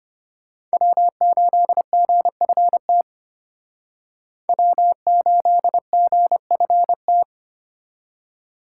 Morse code, heard as a keyed tone, about 32 wpm, 700 Hz. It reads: W8GFT W8GFT